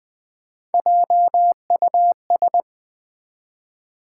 Morse code, heard as a keyed tone, about 20 wpm, 700 Hz. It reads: JUS
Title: Build A Word - 1 Letter At A Time - 4-Letter Words - 20wpm